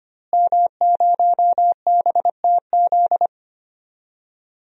Morse code, heard as a keyed tone, about 25 wpm, 700 Hz. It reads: M0BTZ
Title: DXCC-100 Call Signs - 5 Characters - Repeated 3x - 25wpm